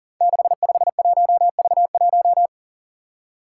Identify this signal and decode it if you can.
Morse code, heard as a keyed tone, about 40 wpm, 700 Hz. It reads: 651V1
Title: Five Challenging Letter-Number Combinations 40wpm